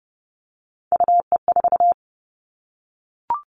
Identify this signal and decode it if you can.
Morse code, heard as a keyed tone, about 30 wpm, 700 Hz. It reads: UE4
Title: Three Challenging Letter-Number Combinations 30wpm